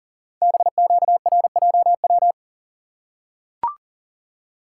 Morse code, heard as a keyed tone, about 40 wpm, 690 Hz. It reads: BQRJW
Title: Code Groups - 5 Letters - 40wpm